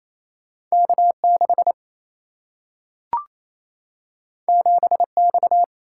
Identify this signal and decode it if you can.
Morse code, heard as a keyed tone, about 28 wpm, 700 Hz. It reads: K6 7X
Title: Single-Letter Single-Number 28wpm